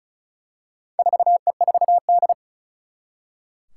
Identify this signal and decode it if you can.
Morse code, heard as a keyed tone, about 35 wpm, 700 Hz. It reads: 4E4D
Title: Four Challenging Letter-Number Combinations 35wpm